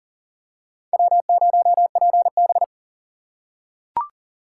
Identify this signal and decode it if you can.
Morse code, heard as a keyed tone, about 40 wpm, 700 Hz. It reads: W0PB
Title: USA Call Signs - 2x Character Spacing - 40wpm